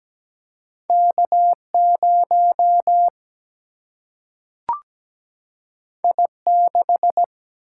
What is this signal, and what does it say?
Morse code, heard as a keyed tone, about 17 wpm, 700 Hz. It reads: K0 I6
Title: Single-Letter Single-Number 17wpm